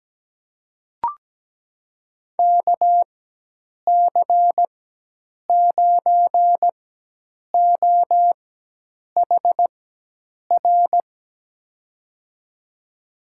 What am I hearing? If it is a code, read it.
Morse code, heard as a keyed tone, about 17 wpm, spaced as for 8 wpm, 700 Hz. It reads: KC9OHR
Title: USA Call Signs - 4x Character Spacing - 17wpm